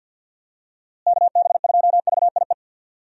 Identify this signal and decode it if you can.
Morse code, heard as a keyed tone, about 47 wpm, 700 Hz. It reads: KB2VIE
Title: USA Call Signs - 6 Characters - Repeated 2x - 50wpm